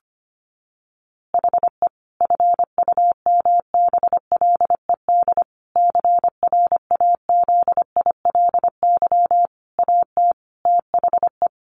Morse code, heard as a keyed tone, about 25 wpm, 700 Hz. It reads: HE FUMBLED CRAZILY AT THE
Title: A Sound Of Thunder 25wpm